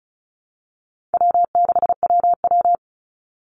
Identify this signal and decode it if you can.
Morse code, heard as a keyed tone, about 34 wpm, 700 Hz. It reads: W6WW
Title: NA SOTA Chaser Callsigns 35wpm